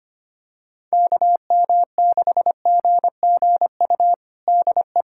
Morse code, heard as a keyed tone, about 25 wpm, 700 Hz. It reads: KM6GGU DE
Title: USA Callsign DE Callsign 25wpm